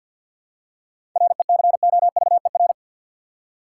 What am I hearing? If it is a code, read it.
Morse code, heard as a keyed tone, about 52 wpm, 700 Hz. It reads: RECOVER
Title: Seven Letter Words - 50wpm